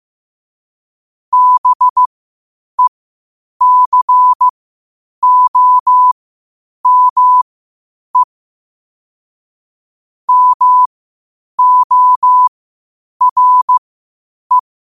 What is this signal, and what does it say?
Morse code, heard as a keyed tone, about 15 wpm, spaced as for 8 wpm, 1000 Hz. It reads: BECOME MORE
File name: qotdfiles / 8wpm